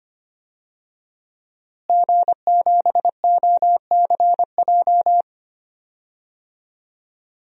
Morse code, heard as a keyed tone, about 25 wpm, 700 Hz. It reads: G7OCJ
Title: DXCC-100 Call Signs - 5 Characters - Repeated 3x - 25wpm